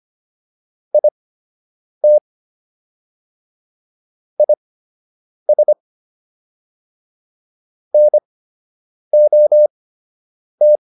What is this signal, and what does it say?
Morse code, heard as a keyed tone, about 25 wpm, spaced as for 8 wpm, 600 Hz. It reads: IT IS NOT